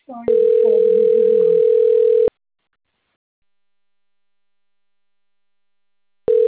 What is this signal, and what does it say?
Ringback tone (North America), 440 + 480 Hz, with speech.
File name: rg-1600-unknown